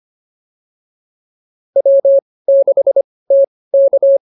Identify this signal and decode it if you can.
Morse code, heard as a keyed tone, about 25 wpm, spaced as for 18 wpm, 550 Hz. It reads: W6TK
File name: POTA106 18